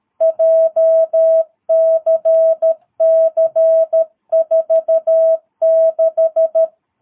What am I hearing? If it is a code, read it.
Morse code, heard as a keyed tone, about 13 wpm, 650 Hz. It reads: JCC46
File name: jh8cbh / 2022cw